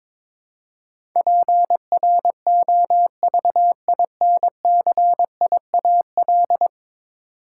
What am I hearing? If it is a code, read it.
Morse code, heard as a keyed tone, about 22 wpm, 700 Hz. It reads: PROVINCIAL
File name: practice / 22wpm / 10 Letter Words - First 3 Letters Given - 22wpm